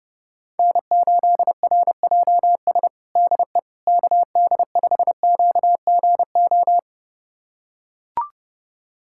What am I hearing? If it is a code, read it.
Morse code, heard as a keyed tone, about 30 wpm, 700 Hz. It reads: N8RJS DE KD5QGO
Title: USA Callsign DE Callsign 30wpm